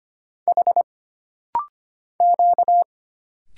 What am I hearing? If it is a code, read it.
Morse code, heard as a keyed tone, about 25 wpm, 700 Hz. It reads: H Q
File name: Single Character (Letter:Number:Punctuation:Prosign) - Rapid-Fire - 25wpm